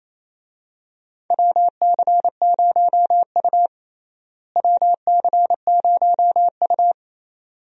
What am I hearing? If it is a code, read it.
Morse code, heard as a keyed tone, about 27 wpm, 700 Hz. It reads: WC0U WC0U
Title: USA Call Signs - 4 Characters - Repeated 2x - 28wpm